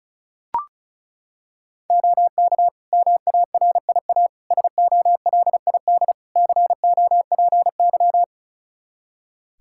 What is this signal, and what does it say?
Morse code, heard as a keyed tone, about 36 wpm, 700 Hz. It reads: OK MARIA SOLID COPY
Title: Practice QSO - Exchange 3 - More Info - 35wpm